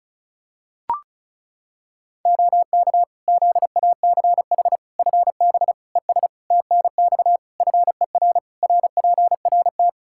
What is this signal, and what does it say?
Morse code, heard as a keyed tone, about 34 wpm, 700 Hz. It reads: OK ZACH FB ES TNX FER RPRT
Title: K1USN SST QSO Practice 35wpm